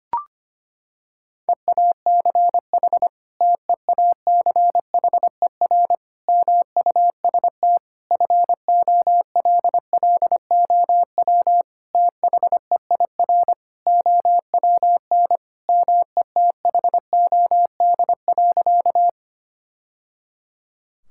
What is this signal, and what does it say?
Morse code, heard as a keyed tone, about 25 wpm, 700 Hz. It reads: EACH TEACHER MUST FOLLOW THEIR OWN METHOD.
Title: Sentences from Top 1000 Words - Review Part 6 - 25wpm